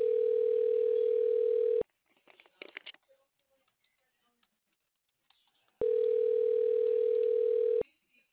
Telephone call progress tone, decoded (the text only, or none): ringback tone